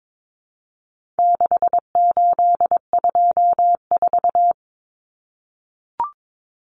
Morse code, heard as a keyed tone, about 22 wpm, 700 Hz. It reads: 6824
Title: Common Cut Numbers - 4-Digit 22wpm